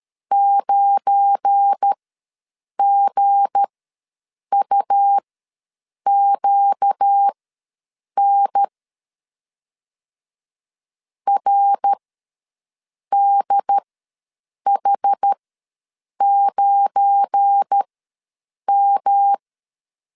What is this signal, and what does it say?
Morse code, heard as a keyed tone, about 12 wpm, spaced as for 7 wpm, 800 Hz. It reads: 9GUQN RDH9M
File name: LEZIONI / 7 GF9Z / 7 GF9 ws 12wpm 3344